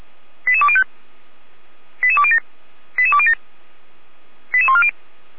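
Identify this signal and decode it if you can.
Five-tone selective call (ZVEI), heard as quick runs of five tones, tones of about 70 ms. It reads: 88205 88207 88207 88140
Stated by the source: ZVEI five-tone selective call received by SDR.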